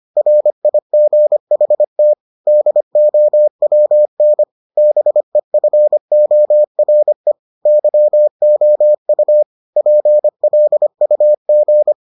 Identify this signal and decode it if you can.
Morse code, heard as a keyed tone, about 25 wpm, 600 Hz. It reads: RIGHT DOWN BEFORE YOU PLUG